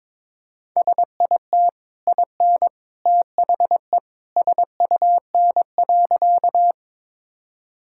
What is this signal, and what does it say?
Morse code, heard as a keyed tone, about 22 wpm, 690 Hz. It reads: SIT IN THE SUN.
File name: practice / 22wpm / Easy Sentences with Hints - 22wpm